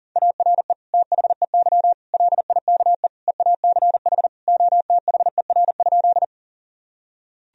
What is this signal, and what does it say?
Morse code, heard as a keyed tone, about 37 wpm, 700 Hz. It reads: ARE THEY LIKE EACH OTHER?
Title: Sentences from Top 1000 Words - Review Part 1 - 40wpm